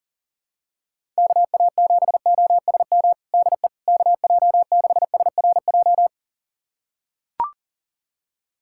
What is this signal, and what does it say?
Morse code, heard as a keyed tone, about 40 wpm, 700 Hz. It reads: KA7OSM DE KJ6SRJ
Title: USA Callsign DE Callsign 40wpm